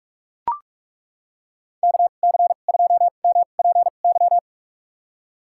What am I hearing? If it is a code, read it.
Morse code, heard as a keyed tone, about 41 wpm, spaced as for 31 wpm, 700 Hz. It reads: KC2MPY
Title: USA Call Signs - 2x Character Spacing - 45wpm